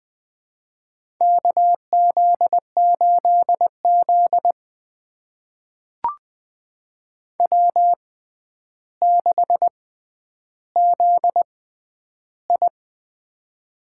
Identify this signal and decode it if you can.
Morse code, heard as a keyed tone, about 20 wpm, spaced as for 10 wpm, 700 Hz. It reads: KZ8Z W6ZI
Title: USA Call Signs - 4 Characters - 6x Character Spacing - 20wpm